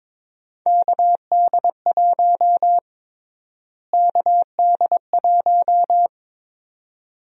Morse code, heard as a keyed tone, about 21 wpm, 700 Hz. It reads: KD1 KD1
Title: USA Call Sign Prefixes - Repeated 2x - 22wpm